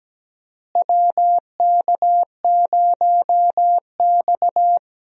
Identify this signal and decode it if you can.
Morse code, heard as a keyed tone, about 17 wpm, 700 Hz. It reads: WK0X
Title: USA Call Signs - 4 Characters - Repeated 2x - 17wpm